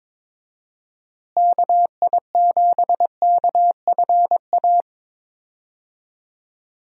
Morse code, heard as a keyed tone, about 22 wpm, 700 Hz. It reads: KI7KFA
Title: USA Call Signs - 6 Characters - Repeated 3x - 22wpm